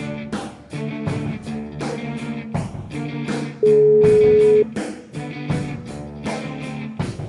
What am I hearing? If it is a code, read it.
Ringback tone (Europe, 425 Hz), with noise.